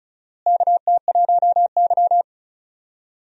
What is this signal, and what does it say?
Morse code, heard as a keyed tone, about 36 wpm, 700 Hz. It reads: KT1Y
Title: USA Call Signs - 4 Characters - Repeated 2x - 35wpm